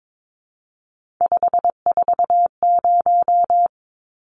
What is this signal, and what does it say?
Morse code, heard as a keyed tone, about 22 wpm, 700 Hz. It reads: 540
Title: Numbers - 3-Digit 22wpm